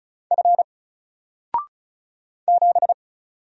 Morse code, heard as a keyed tone, about 35 wpm, 700 Hz. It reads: F 7
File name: Single-Letter-Number-35wpm